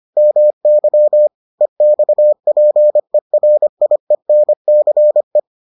Morse code, heard as a keyed tone, about 25 wpm, 600 Hz. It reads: MY EXPERIENCE